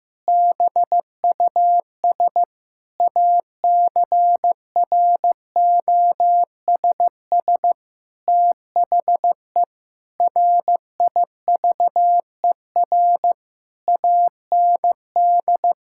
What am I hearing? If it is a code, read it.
Morse code, heard as a keyed tone, about 15 wpm, 700 Hz. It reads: BUS ACROSS THE RIVER AND
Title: Sentences from Top 1000 Words - Review Part 6 - 15wpm